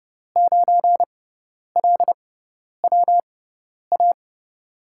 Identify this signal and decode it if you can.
Morse code, heard as a keyed tone, about 30 wpm, spaced as for 10 wpm, 700 Hz. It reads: 9LWA
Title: USA Call Signs - 5 Characters - 6x Character Spacing - 30wpm